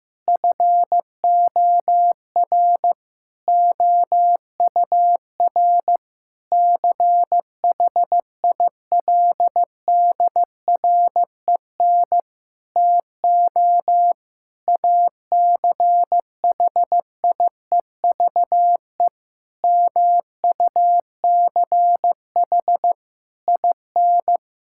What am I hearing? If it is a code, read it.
Morse code, heard as a keyed tone, about 16 wpm, 700 Hz. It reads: FOR OUR CHILDREN TO ACHIEVE MUCH IN
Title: Sentences from Top 1000 Words - Review Part 6 - 15wpm